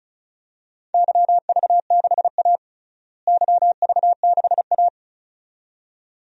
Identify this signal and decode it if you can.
Morse code, heard as a keyed tone, about 36 wpm, 700 Hz. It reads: YV6A YV6A
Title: DXCC-100 Call Signs - 4 Characters - Repeated 2x - 35wpm